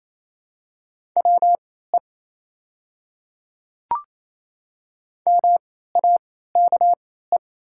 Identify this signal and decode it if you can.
Morse code, heard as a keyed tone, about 28 wpm, spaced as for 16 wpm, 700 Hz. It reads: WE MAKE
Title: 3x Character Spacing - Top 100 Words - 28wpm